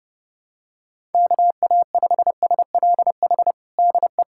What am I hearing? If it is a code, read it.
Morse code, heard as a keyed tone, about 30 wpm, 700 Hz. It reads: KA5SLH DE
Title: USA Callsign DE Callsign 30wpm